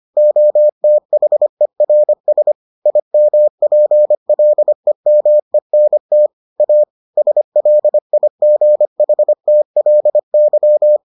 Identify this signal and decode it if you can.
Morse code, heard as a keyed tone, about 25 wpm, 600 Hz. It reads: OTHERS IMPLEMENT A SLIGHTLY